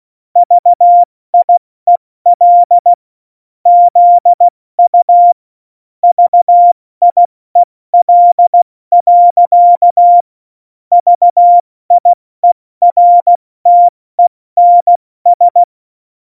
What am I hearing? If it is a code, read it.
Morse code, heard as a keyed tone, about 16 wpm, spaced as for 14 wpm, 700 Hz. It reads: VIEL ZU VIEL. VIERTENS
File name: Morsekurs / Witze / Witz 26. Januar 2021, 70 F80